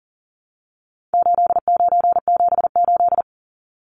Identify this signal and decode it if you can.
Morse code, heard as a keyed tone, about 40 wpm, 700 Hz. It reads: 8978
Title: Numbers - 4-Digit 40wpm